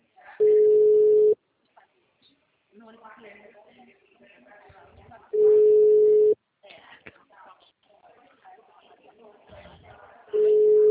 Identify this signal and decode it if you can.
Ringback tone (Europe, 425 Hz), with speech.
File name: out- -2012- - - .57